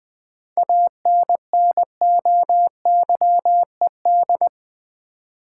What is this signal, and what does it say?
Morse code, heard as a keyed tone, about 20 wpm, 700 Hz. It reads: ANNOYED
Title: 7 Letter Words - First 3 Letters Given - 20wpm